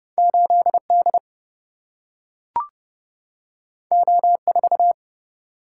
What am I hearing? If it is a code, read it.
Morse code, heard as a keyed tone, about 30 wpm, 700 Hz. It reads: 8 D O 4
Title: Single-Letter Single-Number 30wpm